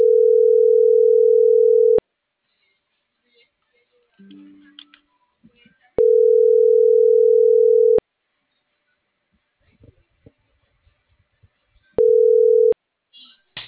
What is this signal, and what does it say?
Ringback tone (North America), 440 + 480 Hz.